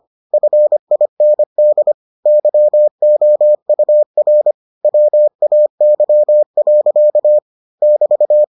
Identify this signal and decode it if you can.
Morse code, heard as a keyed tone, about 25 wpm, 600 Hz. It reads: FIND YOUR WAY. <BT>